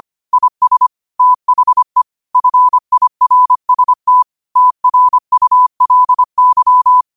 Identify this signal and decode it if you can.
Morse code, heard as a keyed tone, about 25 wpm, 1000 Hz. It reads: IS THE FIRST TRULY
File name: qotdfiles / 25wpm